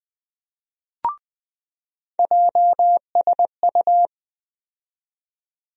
Morse code, heard as a keyed tone, about 20 wpm, 700 Hz. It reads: JSU
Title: USA Call Sign Postfixes - 3 Letters 20wpm